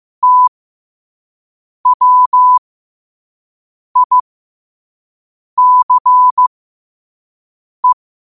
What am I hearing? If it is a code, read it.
Morse code, heard as a keyed tone, about 15 wpm, spaced as for 5 wpm, 1000 Hz. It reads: TWICE